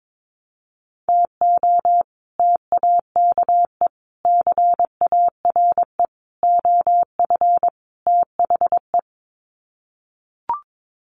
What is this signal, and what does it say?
Morse code, heard as a keyed tone, about 22 wpm, 700 Hz. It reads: TO TAKE CARE OF THE
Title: Sets of 5 Words Encore 22wpm